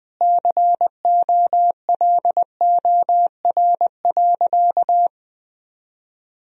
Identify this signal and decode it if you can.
Morse code, heard as a keyed tone, about 20 wpm, 700 Hz. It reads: COLOR.
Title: Sentences from Top 1000 Words - Review Part 2 - 20wpm